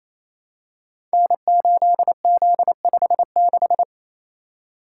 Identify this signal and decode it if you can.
Morse code, heard as a keyed tone, about 28 wpm, 700 Hz. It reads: N8Z56
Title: Code Groups - 5 Letter:Numbers - 28wpm